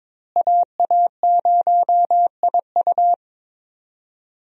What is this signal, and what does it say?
Morse code, heard as a keyed tone, about 22 wpm, 700 Hz. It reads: AA0IU
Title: USA Call Signs - 5 Characters - Repeated 2x - 22wpm